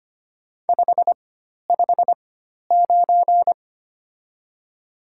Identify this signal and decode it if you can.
Morse code, heard as a keyed tone, about 25 wpm, spaced as for 12 wpm, 690 Hz. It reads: 559
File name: Top 90 QSO Elements - 4x Character Spacing - 25wpm